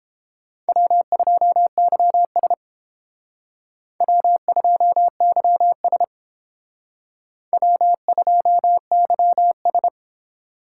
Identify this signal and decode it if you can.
Morse code, heard as a keyed tone, about 29 wpm, 700 Hz. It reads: W2YS W2YS W2YS